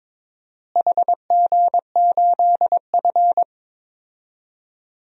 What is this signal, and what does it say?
Morse code, heard as a keyed tone, about 22 wpm, 700 Hz. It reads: HG8F
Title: DXCC-100 Call Signs - 4 Characters - Repeated 2x - 22wpm